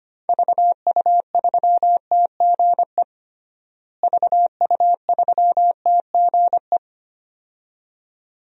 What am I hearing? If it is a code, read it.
Morse code, heard as a keyed tone, about 25 wpm, 700 Hz. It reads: VU3TGE VU3TGE
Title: DXCC-100 Call Signs - 6 Characters - Repeated 2x - 25wpm